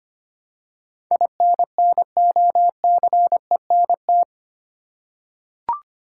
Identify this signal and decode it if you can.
Morse code, heard as a keyed tone, about 25 wpm, 700 Hz. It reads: INNOCENT
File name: Eight Letter Words - 25wpm